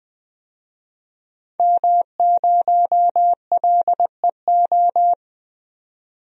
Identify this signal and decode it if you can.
Morse code, heard as a keyed tone, about 20 wpm, 700 Hz. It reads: M0LEO
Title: DXCC-100 Call Signs - 5 Characters - 20wpm